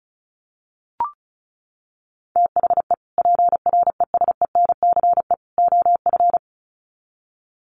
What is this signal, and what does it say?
Morse code, heard as a keyed tone, about 34 wpm, 700 Hz. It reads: THE PRESENCE OF
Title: Sets of 3 Words 35wpm